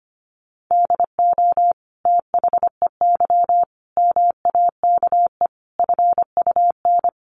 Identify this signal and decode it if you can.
Morse code, heard as a keyed tone, about 25 wpm, 700 Hz. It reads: DO THEY MAKE FUN
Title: Sentences from Top 1300 Words 25wpm